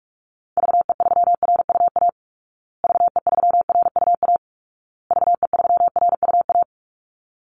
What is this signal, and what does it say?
Morse code, heard as a keyed tone, about 44 wpm, 700 Hz. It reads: VE3RUA VE3RUA VE3RUA
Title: Call Signs Repeated 2x - 45wpm